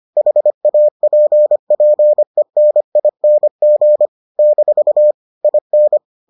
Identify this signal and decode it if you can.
Morse code, heard as a keyed tone, about 25 wpm, 600 Hz. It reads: HAPPENING - IN